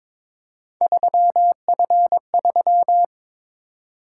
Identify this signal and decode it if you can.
Morse code, heard as a keyed tone, about 22 wpm, 700 Hz. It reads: 3F3
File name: practice / 22wpm / Three Letter-Number Combinations - 22wpm